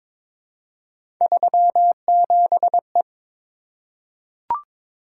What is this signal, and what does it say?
Morse code, heard as a keyed tone, about 22 wpm, 700 Hz. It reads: 37E
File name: practice / 22wpm / Three Letter-Number Combinations - 22wpm